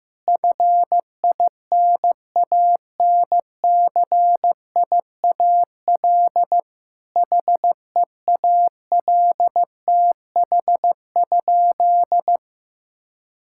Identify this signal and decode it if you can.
Morse code, heard as a keyed tone, about 15 wpm, 700 Hz. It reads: FINANCIAL HEALTH?
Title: Sentences from Top 1000 Words - Review Part 5 - 15wpm